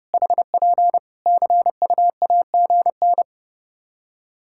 Morse code, heard as a keyed tone, about 30 wpm, 700 Hz. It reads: HP CUAGN
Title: K1USN SST QSO Practice 30wpm